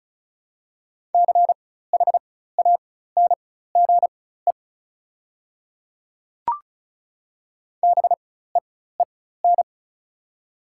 Morse code, heard as a keyed tone, about 36 wpm, spaced as for 16 wpm, 690 Hz. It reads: CHANGE BEEN
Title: 4x Character Spacing - Top 500 Words - 35wpm